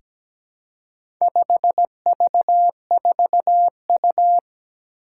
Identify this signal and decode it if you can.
Morse code, heard as a keyed tone, about 17 wpm, 700 Hz. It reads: 5V4U
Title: Four Challenging Letter-Number Combinations 17wpm